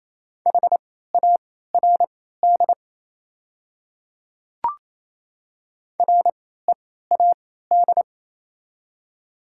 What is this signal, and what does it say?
Morse code, heard as a keyed tone, about 28 wpm, spaced as for 16 wpm, 700 Hz. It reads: HARD READ
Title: 3x Character Spacing - Top 500 Words - 28wpm